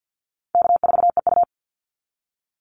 Morse code, heard as a keyed tone, about 50 wpm, 700 Hz. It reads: K4EU
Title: Easy Call Signs - 50wpm